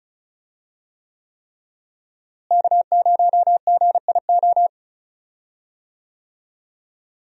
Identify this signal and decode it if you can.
Morse code, heard as a keyed tone, about 36 wpm, 700 Hz. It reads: K0GIO
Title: USA Call Signs - 5 Characters - Repeated 3x - 35wpm